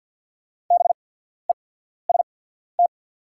Morse code, heard as a keyed tone, about 48 wpm, spaced as for 14 wpm, 700 Hz. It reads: BEST
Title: 8x Character Spacing - Top 500 Words - 50wpm